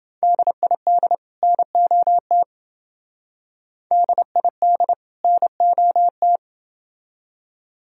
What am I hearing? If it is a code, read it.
Morse code, heard as a keyed tone, about 28 wpm, 700 Hz. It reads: DID NOT DID NOT